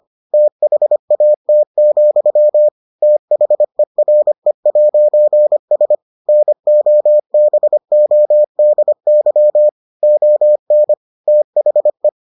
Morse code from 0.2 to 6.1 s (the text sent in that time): THAT, THERE'S